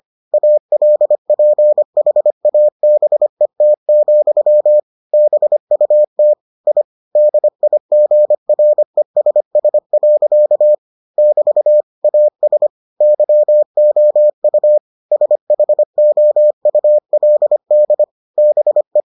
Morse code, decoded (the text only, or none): ALPHABET, BUT I DIGRESS. <BT> AS YOU SHOULD BE